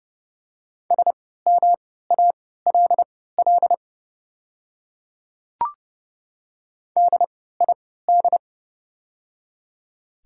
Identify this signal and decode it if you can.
Morse code, heard as a keyed tone, about 31 wpm, spaced as for 17 wpm, 700 Hz. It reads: SMALL DID
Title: 3x Character Spacing - Top 500 Words - 30wpm